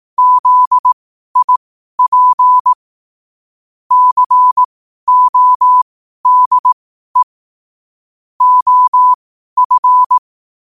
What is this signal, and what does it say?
Morse code, heard as a keyed tone, about 18 wpm, spaced as for 12 wpm, 1000 Hz. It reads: ZIP CODE OF
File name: qotdfiles / 12wpm